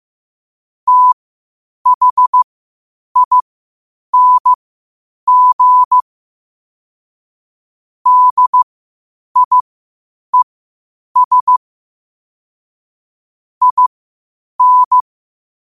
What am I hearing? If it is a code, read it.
Morse code, heard as a keyed tone, about 15 wpm, spaced as for 8 wpm, 1000 Hz. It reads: THING DIES IN
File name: qotdfiles / 8wpm